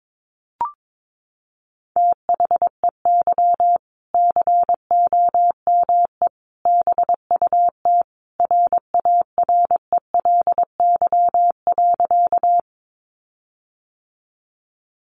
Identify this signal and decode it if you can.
Morse code, heard as a keyed tone, about 22 wpm, 700 Hz. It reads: THEY COME BUT RARELY.
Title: Sentences from Top 1400 Words 22wpm